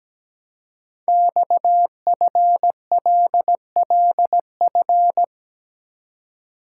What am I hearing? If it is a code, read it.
Morse code, heard as a keyed tone, about 17 wpm, 700 Hz. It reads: XFLLF
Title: Code Groups - 5 Letters - 17wpm